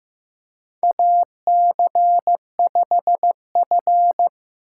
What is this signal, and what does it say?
Morse code, heard as a keyed tone, about 15 wpm, 700 Hz. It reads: AC5F
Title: USA Call Signs - 4 Characters - 3x Character Spacing - 15wpm